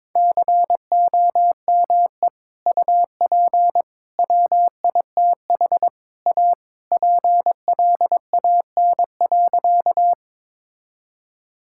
Morse code, heard as a keyed tone, about 22 wpm, 700 Hz. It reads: COME UP WITH A PLAN.
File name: practice / 22wpm / Sentences with 4-Word Phrases - 22wpm